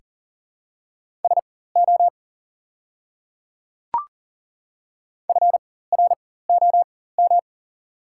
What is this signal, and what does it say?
Morse code, heard as a keyed tone, about 40 wpm, spaced as for 19 wpm, 700 Hz. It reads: SO FROM
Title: 4x Character Spacing - Top 100 Words - 40wpm